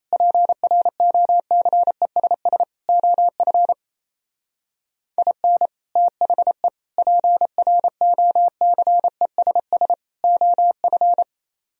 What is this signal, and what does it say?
Morse code, heard as a keyed tone, about 35 wpm, 690 Hz. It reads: PROCESS OF IN THE PROCESS OF